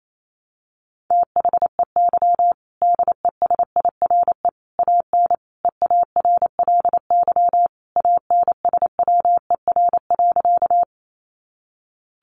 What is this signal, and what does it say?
Morse code, heard as a keyed tone, about 28 wpm, 700 Hz. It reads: THEY DESIRE AN EARLY ANSWER.
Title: Sentences from Top 1100 Words 28wpm